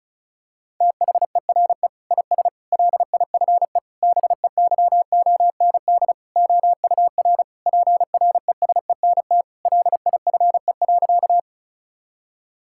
Morse code, heard as a keyed tone, about 36 wpm, 700 Hz. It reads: THERE IS LIFE BEYOND OUR PRESENT LIFE.
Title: Sentences from Top 1000 Words - Review Part 6 - 35wpm